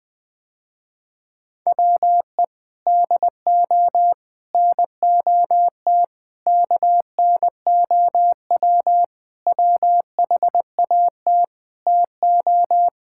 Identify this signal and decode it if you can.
Morse code, heard as a keyed tone, about 21 wpm, 700 Hz. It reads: WE DO NOT KNOW WHAT TO
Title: Sentences with 5-Word Phrases - Spoken Beginning - 20wpm